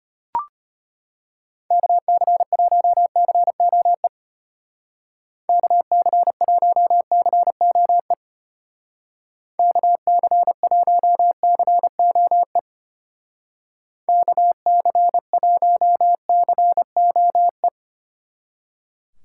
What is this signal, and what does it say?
Morse code, heard as a keyed tone, about 32 wpm, 700 Hz. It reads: KC1COE KC1COE KC1COE KC1COE